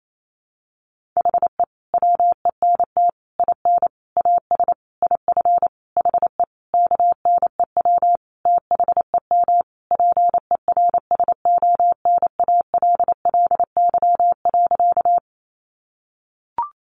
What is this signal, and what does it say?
Morse code, heard as a keyed tone, about 28 wpm, 700 Hz. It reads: HE WENT IN AS IF HE KNEW THEM PERSONALLY.
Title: Sentences from Top 1400 Words 28wpm